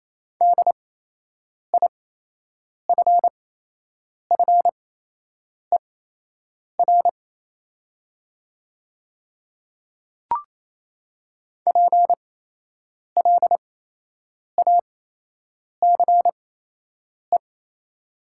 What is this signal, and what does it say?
Morse code, heard as a keyed tone, about 28 wpm, spaced as for 8 wpm, 700 Hz. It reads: DIFFER PLACE
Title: 8x Character Spacing - Top 300 Words - 28wpm